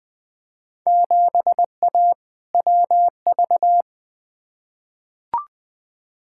Morse code, heard as a keyed tone, about 20 wpm, 700 Hz. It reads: 7A WV
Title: Field Day Exchanges Encore - 20wpm